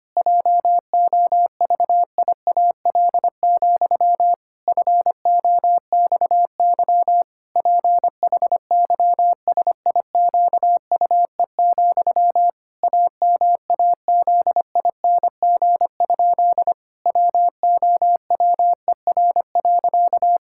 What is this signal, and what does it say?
Morse code, heard as a keyed tone, about 25 wpm, 700 Hz. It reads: JOVIAL, FOXY PHYSIQUE, AMAZING? WOWER.
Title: Pangram Challenge v2 - 25wpm